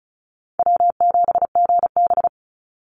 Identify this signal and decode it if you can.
Morse code, heard as a keyed tone, about 34 wpm, 700 Hz. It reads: W7GB
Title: Easy Call Signs Repeated 3x - 35wpm